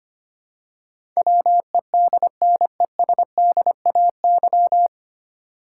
Morse code, heard as a keyed tone, about 25 wpm, 700 Hz. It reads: WEDNESDAY
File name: practice / 25wpm / 9 Letter Words - First 3 Letters Given - 25wpm